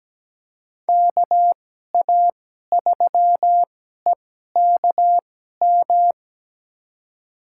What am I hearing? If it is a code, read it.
Morse code, heard as a keyed tone, about 17 wpm, spaced as for 12 wpm, 700 Hz. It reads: KA3EKM
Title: USA Call Signs - 6 Characters - 2x Character Spacing - 17wpm